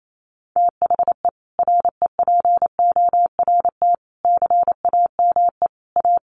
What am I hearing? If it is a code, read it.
Morse code, heard as a keyed tone, about 28 wpm, 700 Hz. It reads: THE REPORT CAME A
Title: Sentences from Top 900 Words 28wpm